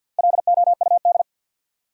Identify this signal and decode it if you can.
Morse code, heard as a keyed tone, about 53 wpm, 700 Hz. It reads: LOUD
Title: Four Letter Words - 50wpm